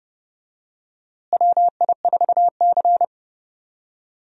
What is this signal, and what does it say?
Morse code, heard as a keyed tone, about 32 wpm, 700 Hz. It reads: WI4C
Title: USA Call Signs - 4 Characters - Repeated 3x - 30wpm